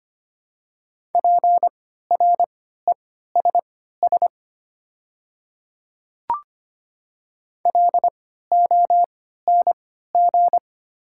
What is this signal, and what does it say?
Morse code, heard as a keyed tone, about 25 wpm, spaced as for 14 wpm, 700 Hz. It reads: PRESS LONG
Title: 3x Character Spacing - Top 300 Words - 25wpm